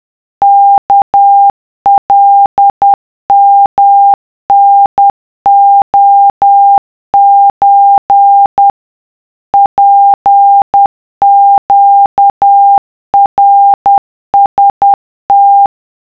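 Morse code (Japanese Wabun code), heard as a keyed tone, about 10 wpm, 800 Hz. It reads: ワカヨタレソ ツネナラム
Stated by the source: Japanese wabun Morse of the iroha poem, generated as a wav by a text-to-Morse app at 50 CPM (10 WPM).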